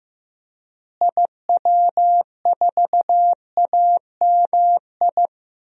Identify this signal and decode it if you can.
Morse code, heard as a keyed tone, about 15 wpm, 700 Hz. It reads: IW4AMI
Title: DXCC-100 Call Signs - 6 Characters - Repeated 3x - 15wpm